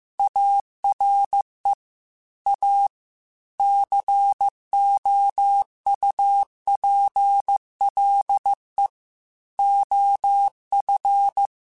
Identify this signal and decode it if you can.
Morse code, heard as a keyed tone, about 15 wpm, 800 Hz. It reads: ARE A COUPLE OF
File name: Technical Subjects / Technical Subjects-mp3s / Sw Pwr Sup @13